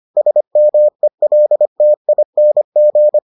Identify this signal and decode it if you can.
Morse code, heard as a keyed tone, about 25 wpm, 600 Hz. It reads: SMELTING